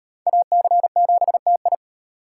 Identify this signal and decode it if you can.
Morse code, heard as a keyed tone, about 38 wpm, 700 Hz. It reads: AC7TI